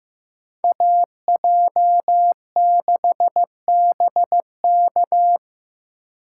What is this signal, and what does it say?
Morse code, heard as a keyed tone, about 18 wpm, 700 Hz. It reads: AJ6BK
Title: Speed Racing - USA 2x2 Call Signs 15wpm